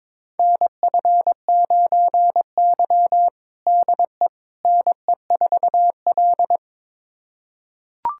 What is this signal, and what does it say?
Morse code, heard as a keyed tone, about 21 wpm, 700 Hz. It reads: NF9Y DE NE4L
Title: USA Callsign DE Callsign - 4 Characters - 22wpm